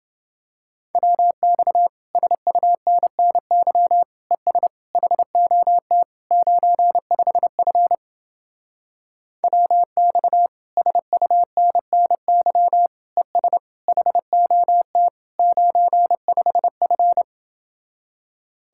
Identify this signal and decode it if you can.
Morse code, heard as a keyed tone, about 29 wpm, 700 Hz. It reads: WX SUNNY ES HOT 95F WX SUNNY ES HOT 95F